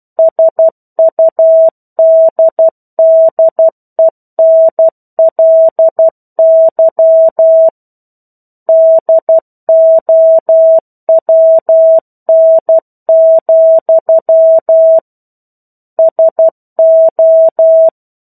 Morse code, heard as a keyed tone, about 12 wpm, 650 Hz. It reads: SUDDENLY DOWN, SO
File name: 01-Chapter-01 12 WPM